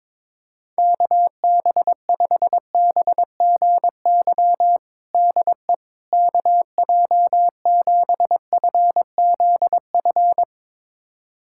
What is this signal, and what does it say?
Morse code, heard as a keyed tone, about 22 wpm, 700 Hz. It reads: KB5BGY DE KJ7FZF
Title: USA Callsign DE Callsign 22wpm